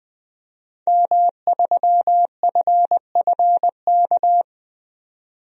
Morse code, heard as a keyed tone, about 20 wpm, 700 Hz. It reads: M3FFK
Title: DXCC-100 Call Signs - 5 Characters - 20wpm